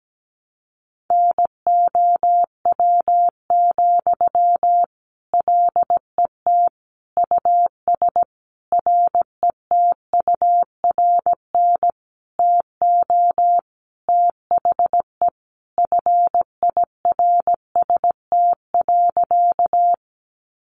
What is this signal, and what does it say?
Morse code, heard as a keyed tone, about 17 wpm, 700 Hz. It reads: NOW, LET US RETURN TO THE FIRST.
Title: Sentences from Top 800 Words 17wpm